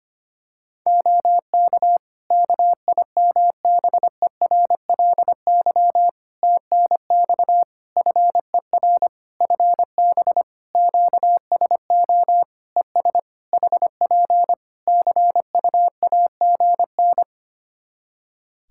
Morse code, heard as a keyed tone, about 25 wpm, 700 Hz. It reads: OK KIMBERLY TNX FER FB QSO ES HP CUAGN
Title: Practice QSO - Exchange 4 - Ending - 25wpm